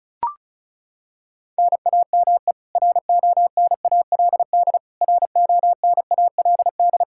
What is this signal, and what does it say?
Morse code, heard as a keyed tone, about 34 wpm, 700 Hz. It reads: NAME RONALD RONALD
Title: Practice QSO - Exchange 1 - Beginning - 35wpm